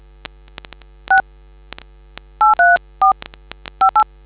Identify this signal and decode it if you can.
DTMF, heard as telephone touch tones, tones of about 120 ms. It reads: 6 834 58